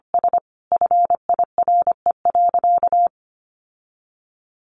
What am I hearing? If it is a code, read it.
Morse code, heard as a keyed tone, about 25 wpm, 700 Hz. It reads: S FIRE.
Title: Sentences from Top 1500 Words 25wpm